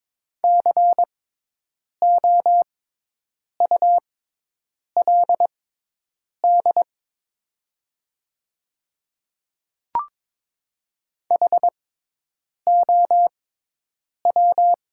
Morse code, heard as a keyed tone, about 22 wpm, spaced as for 8 wpm, 700 Hz. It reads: COULD HOW
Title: 6x Character Spacing - Top 200 Words - 22wpm